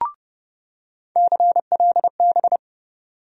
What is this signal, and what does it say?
Morse code, heard as a keyed tone, about 31 wpm, 700 Hz. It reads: CLB
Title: USA Call Sign Postfixes - 3 Letters 30wpm